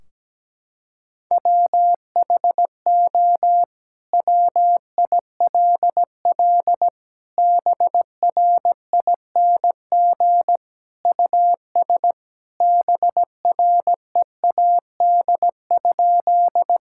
Morse code, heard as a keyed tone, about 17 wpm, 700 Hz. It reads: WHO WILL BRING US BREAD?
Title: Sentences from Top 1600 Words 17wpm